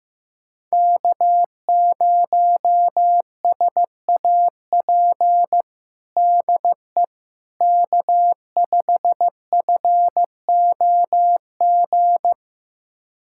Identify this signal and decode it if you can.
Morse code, heard as a keyed tone, about 16 wpm, 700 Hz. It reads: K0SAP DE K5FOG
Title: USA Callsign DE Callsign - 5 Characters - 15wpm